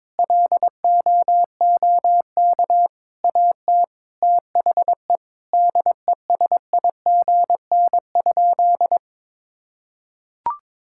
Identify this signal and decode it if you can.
Morse code, heard as a keyed tone, about 22 wpm, 700 Hz. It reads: LOOK AT THE DESIGN?
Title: Sentences from Top 1000 Words - Review Part 5 - 22wpm